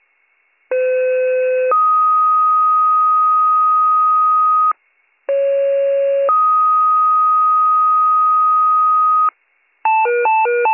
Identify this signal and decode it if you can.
Two-tone page, heard as a one-second tone, then a three-second tone, about 510 then 1250 Hz, about 550 then 1250 Hz. It is heard from 0.7 to 9.3 s.